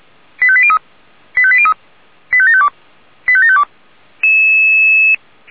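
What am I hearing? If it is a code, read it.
Five-tone selective call (ZVEI), heard as quick runs of five tones, tones of about 70 ms. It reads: 85803 85803 85752 85752
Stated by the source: ZVEI five-tone call received by SDR.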